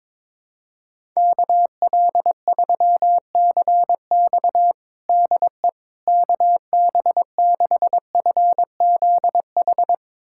Morse code, heard as a keyed tone, about 21 wpm, 700 Hz. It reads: KL3CX DE KB6FZH
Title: USA Callsign DE Callsign 22wpm